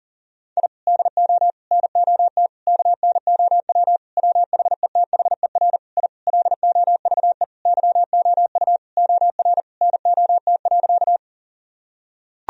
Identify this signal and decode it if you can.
Morse code, heard as a keyed tone, about 40 wpm, 700 Hz. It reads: I DO NOT KNOW WHETHER I LOVE YOU OR NOT.
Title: Sentences from Top 1000 Words - Review Part 3 - 40wpm